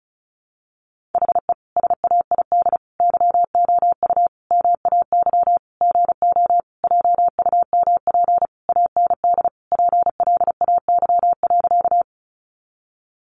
Morse code, heard as a keyed tone, about 34 wpm, 700 Hz. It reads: HE SAID YOU MAY GO JUMP AND PLAY.
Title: Sentences from Top 1300 Words 35wpm